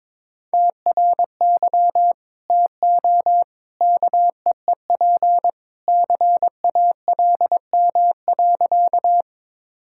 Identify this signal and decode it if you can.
Morse code, heard as a keyed tone, about 21 wpm, 700 Hz. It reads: TRY TO KEEP CALM.
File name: practice / 22wpm / Sentences with 2-Word Phrases - 22wpm